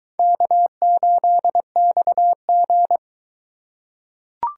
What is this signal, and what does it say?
Morse code, heard as a keyed tone, about 21 wpm, 700 Hz. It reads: K8XG